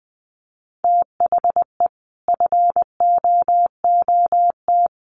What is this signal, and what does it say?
Morse code, heard as a keyed tone, about 20 wpm, 700 Hz. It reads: THE FOOT
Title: Sets of 5 Words Encore2 20wpm